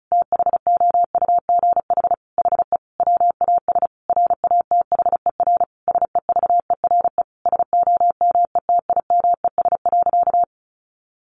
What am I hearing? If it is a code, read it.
Morse code, heard as a keyed tone, about 34 wpm, 700 Hz. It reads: THOUGH HE WAS RATHER SEVERE SOMETIMES.
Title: Sentences from Top 1400 Words 35wpm